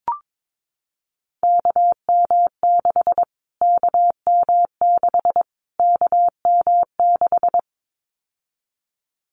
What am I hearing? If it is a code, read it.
Morse code, heard as a keyed tone, about 22 wpm, 700 Hz. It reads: KM6 KM6 KM6
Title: Callsign Prefixes for Top 25 DXCC - Repeated 3x - 22wpm